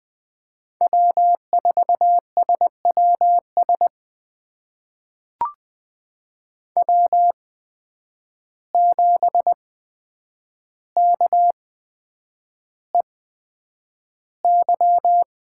Morse code, heard as a keyed tone, about 20 wpm, spaced as for 9 wpm, 700 Hz. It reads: W4SWS W7KEY
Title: USA Call Signs - 5 Characters - 8x Character Spacing - 20wpm